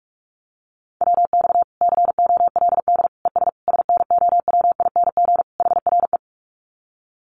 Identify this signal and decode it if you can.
Morse code, heard as a keyed tone, about 44 wpm, 700 Hz. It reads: WX COLD ES SNOWING HRE
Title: Common QSO Phrases 45wpm